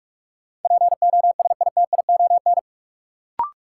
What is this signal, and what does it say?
Morse code, heard as a keyed tone, about 46 wpm, 700 Hz. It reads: POSITION
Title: Sentences from Top 900 Words - New Word Drill - 45wpm